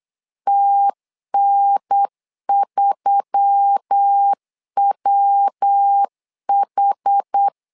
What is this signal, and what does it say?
Morse code, heard as a keyed tone, about 8 wpm, 800 Hz. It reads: TN3WH
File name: LEZIONI / 4 Q1S3 8wpm 3037